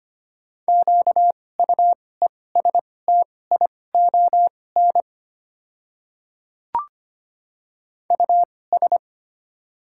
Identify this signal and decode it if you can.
Morse code, heard as a keyed tone, about 25 wpm, spaced as for 18 wpm, 700 Hz. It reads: QUESTION US
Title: 2x Character Spacing - Top 500 Words - 25wpm